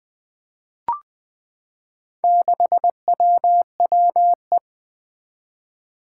Morse code, heard as a keyed tone, about 20 wpm, 700 Hz. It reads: 6WWE
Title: Four Challenging Letter-Number Combinations 20wpm